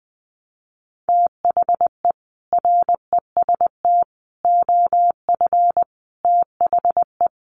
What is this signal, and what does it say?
Morse code, heard as a keyed tone, about 20 wpm, 700 Hz. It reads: THE REST OF THE
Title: Sets of 5 Words 20wpm